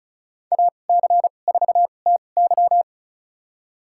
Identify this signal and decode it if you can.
Morse code, heard as a keyed tone, about 35 wpm, spaced as for 25 wpm, 700 Hz. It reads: AC4TY
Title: USA Call Signs - 5 Characters - 2x Character Spacing - 35wpm